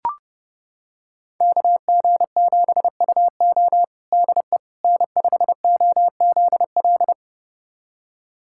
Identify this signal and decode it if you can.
Morse code, heard as a keyed tone, about 30 wpm, 700 Hz. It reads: KG7UO DE N5OZL
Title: USA Callsign DE Callsign - 5 Characters - 30wpm